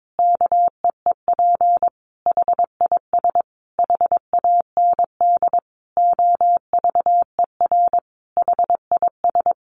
Morse code, heard as a keyed tone, about 22 wpm, 700 Hz. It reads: KEEP HIS HAND OVER HIS
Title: Sentences from Top 1200 Words 22wpm